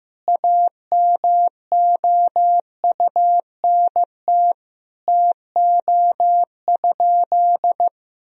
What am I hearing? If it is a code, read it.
Morse code, heard as a keyed tone, about 15 wpm, 700 Hz. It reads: AMOUNT TO?
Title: Sentences from Top 1000 Words - Review Part 4 - 15wpm